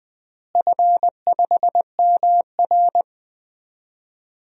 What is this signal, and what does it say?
Morse code, heard as a keyed tone, about 21 wpm, 700 Hz. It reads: F5MR
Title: DXCC-100 Call Signs - 4 Characters - Repeated 3x - 20wpm